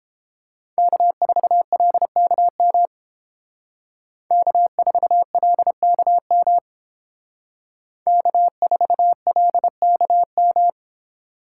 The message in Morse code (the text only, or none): K4LKM K4LKM K4LKM